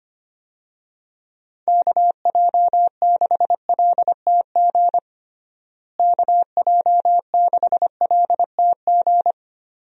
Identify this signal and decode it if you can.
Morse code, heard as a keyed tone, about 25 wpm, 700 Hz. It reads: KJ6LTG KJ6LTG
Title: USA Call Signs - 6 Characters - Repeated 2x - 25wpm